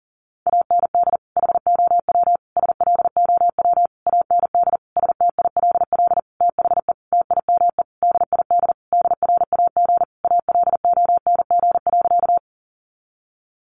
Morse code, heard as a keyed tone, about 40 wpm, 700 Hz. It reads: AND HOW SLOW AND STILL THE TIME DID DRAG ALONG.
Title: Sentences from Top 1500 Words 40wpm